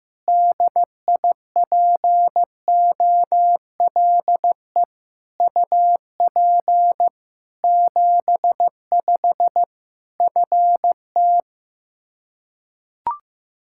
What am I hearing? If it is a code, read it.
Morse code, heard as a keyed tone, about 15 wpm, 700 Hz. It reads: DIPOLE UP 75 FT
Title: Practice QSO - Exchange 2 - Info - 15wpm